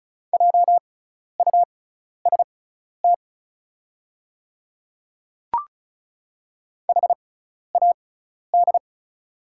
Morse code, heard as a keyed tone, about 36 wpm, spaced as for 12 wpm, 690 Hz. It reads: JUST HAD
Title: 6x Character Spacing - Top 500 Words - 35wpm